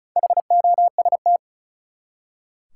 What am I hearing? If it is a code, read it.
Morse code, heard as a keyed tone, about 36 wpm, 700 Hz. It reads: HOST
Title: Four Letter Words - 35wpm